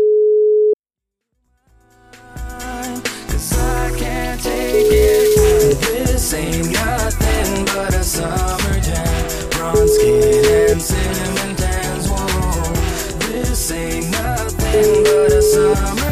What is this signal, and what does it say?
Telephone call progress tone (Europe, 425 Hz), heard as a telephone network's tone, about 420 Hz, with noise. Ringback tone.